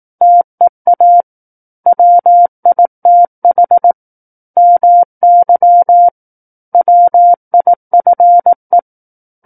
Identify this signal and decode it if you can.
Morse code, heard as a keyed tone, about 18 wpm, 700 Hz. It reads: TEA WITH MY WIFE
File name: 18 WPM